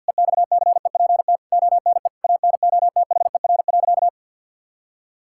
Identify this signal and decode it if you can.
Morse code, heard as a keyed tone, about 50 wpm, 700 Hz. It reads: EXCEPT ONE ANOTHER.
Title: Sentences from Top 1600 Words 50wpm